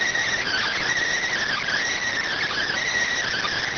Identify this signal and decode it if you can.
Emergency Alert System (SAME) tones, heard as harsh buzzing bursts. It is heard from 0.4 to 2.9 s.